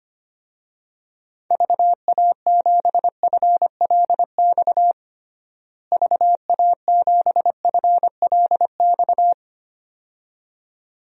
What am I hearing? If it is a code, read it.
Morse code, heard as a keyed tone, about 25 wpm, 700 Hz. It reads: VA7FLX VA7FLX
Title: DXCC-100 Call Signs - 6 Characters - Repeated 2x - 25wpm